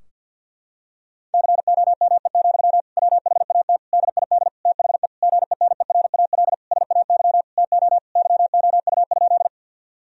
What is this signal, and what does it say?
Morse code, heard as a keyed tone, about 49 wpm, 700 Hz. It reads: COME, WHAT DID THE GENERAL SAY TO YOU?
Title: Sentences from Top 1000 Words - Review Part 3 - 50wpm